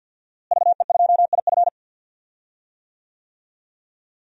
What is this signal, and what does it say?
Morse code, heard as a keyed tone, about 51 wpm, 700 Hz. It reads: VE2IF
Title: DXCC-100 Call Signs - 5 Characters - Repeated 3x - 50wpm